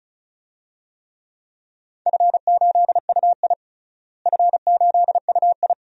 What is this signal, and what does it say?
Morse code, heard as a keyed tone, about 36 wpm, 690 Hz. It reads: F8UI F8UI
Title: DXCC-100 Call Signs - 4 Characters - Repeated 2x - 35wpm